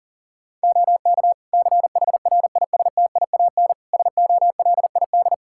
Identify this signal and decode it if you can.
Morse code, heard as a keyed tone, about 40 wpm, 700 Hz. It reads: OK CHRISTIAN SOLID
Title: Practice QSO - Exchange 3 - More Info - 40wpm